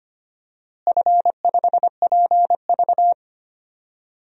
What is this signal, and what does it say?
Morse code, heard as a keyed tone, about 25 wpm, 700 Hz. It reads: F5PV
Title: DXCC-100 Call Signs - 4 Characters - Repeated 2x - 25wpm